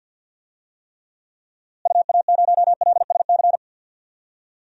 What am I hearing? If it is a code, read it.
Morse code, heard as a keyed tone, about 52 wpm, 700 Hz. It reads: UA0LSC
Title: DXCC-100 Call Signs - 6 Characters - Repeated 3x - 50wpm